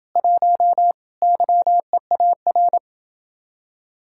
Morse code, heard as a keyed tone, about 30 wpm, 700 Hz. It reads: 1 YEAR